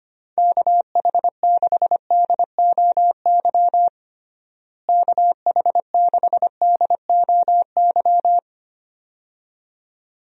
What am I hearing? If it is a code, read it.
Morse code, heard as a keyed tone, about 25 wpm, 700 Hz. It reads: KH6DOY KH6DOY
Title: DXCC-100 Call Signs - 6 Characters - Repeated 2x - 25wpm